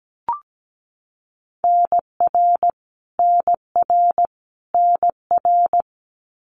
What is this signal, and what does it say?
Morse code, heard as a keyed tone, about 17 wpm, 700 Hz. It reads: NR NR NR
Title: Abbreviations Repeated 3x - 17wpm